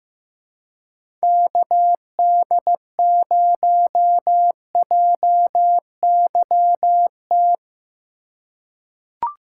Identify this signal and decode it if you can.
Morse code, heard as a keyed tone, about 15 wpm, 700 Hz. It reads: KD0JYT
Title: USA Call Signs - 6 Characters - 4x Character Spacing - 15wpm